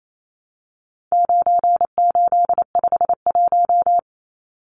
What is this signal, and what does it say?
Morse code, heard as a keyed tone, about 28 wpm, 700 Hz. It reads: 9851
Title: Numbers - 4-Digit 28wpm